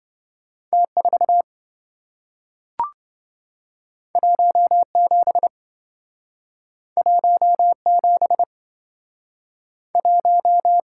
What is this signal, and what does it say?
Morse code, heard as a keyed tone, about 27 wpm, 700 Hz. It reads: T4 17 17 1